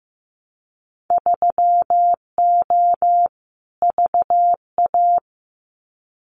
Morse code, heard as a keyed tone, about 15 wpm, 700 Hz. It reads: 3O VA
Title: Winter Field Day Exchanges 15wpm